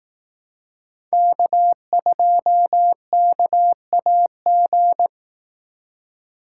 Morse code, heard as a keyed tone, about 18 wpm, 700 Hz. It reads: K2KAG
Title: Speed Racing - USA 1x3 Call Signs 15wpm